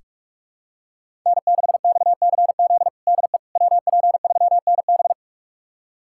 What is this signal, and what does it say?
Morse code, heard as a keyed tone, about 41 wpm, 700 Hz. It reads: N6XCZ DE WP3NB
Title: USA Callsign DE Callsign - 5 Characters - 45wpm